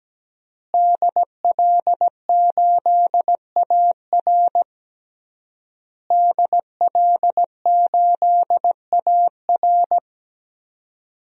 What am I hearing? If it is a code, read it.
Morse code, heard as a keyed tone, about 17 wpm, 700 Hz. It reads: DL8AR DL8AR
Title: DXCC-100 Call Signs - 5 Characters - Repeated 3x - 17wpm